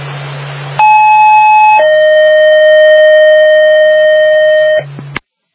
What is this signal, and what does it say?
Two-tone page, heard as a one-second tone, then a three-second tone, about 860 then 620 Hz, with noise. It is heard from 0.8 to 4.8 s.